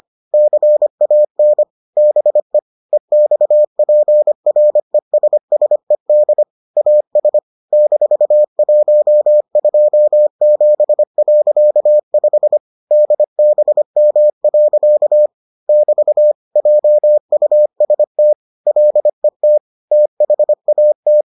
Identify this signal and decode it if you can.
Morse code, heard as a keyed tone, about 25 wpm, 600 Hz. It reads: CAN BE EXPRESSED AS -127.5 DBM. <BT> JUST LET THAT